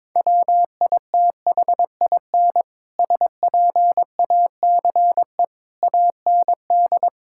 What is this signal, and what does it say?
Morse code, heard as a keyed tone, about 22 wpm, 700 Hz. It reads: WITHIN SPACE AND
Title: Sentences from Top 1000 Words - Review Part 5 - 22wpm